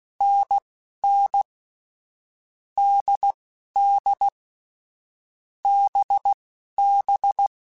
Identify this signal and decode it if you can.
Morse code, heard as a keyed tone, about 16 wpm, spaced as for 11 wpm, 800 Hz. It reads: NN DD BB
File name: Difficult Combos / Difficult Combos-mp3s / NDB NDB6 @10